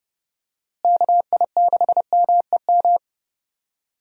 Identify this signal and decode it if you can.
Morse code, heard as a keyed tone, about 30 wpm, 700 Hz. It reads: KI6MEM
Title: USA Call Signs - 6 Characters - Repeated 3x - 30wpm